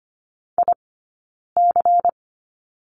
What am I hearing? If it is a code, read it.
Morse code, heard as a keyed tone, about 25 wpm, spaced as for 9 wpm, 690 Hz. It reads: IC
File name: course / 25:10wpm / 092 Lesson 16d - Letter c - Two Characters 25wpm